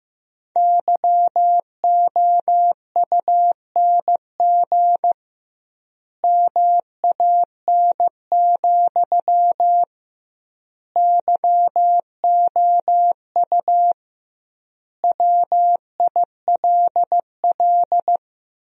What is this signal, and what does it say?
Morse code, heard as a keyed tone, about 15 wpm, 700 Hz. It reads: YOUNG MAN, YOU WILL
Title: 2x Word-Spacing - Sentences from Top 400 Words - 15wpm